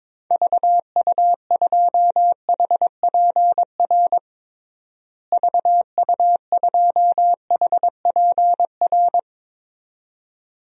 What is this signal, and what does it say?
Morse code, heard as a keyed tone, about 22 wpm, 700 Hz. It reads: VU2HPR VU2HPR
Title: DXCC-100 Call Signs - 6 Characters - Repeated 2x - 22wpm